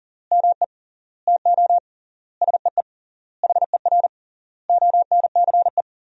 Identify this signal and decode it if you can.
Morse code, heard as a keyed tone, about 40 wpm, 700 Hz. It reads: ME TO SEE HER ONCE
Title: 3x Word-Spacing - Sentences from Top 500 Words - 40wpm